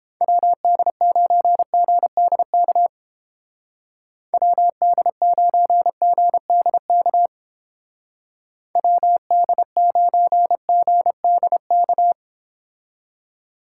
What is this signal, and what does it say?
Morse code, heard as a keyed tone, about 30 wpm, 700 Hz. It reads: WD9GDK WD9GDK WD9GDK